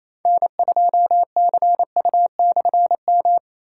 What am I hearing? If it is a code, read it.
Morse code, heard as a keyed tone, about 28 wpm, 700 Hz. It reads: N2CU/M
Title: K1USN SST QSO Practice - 28wpm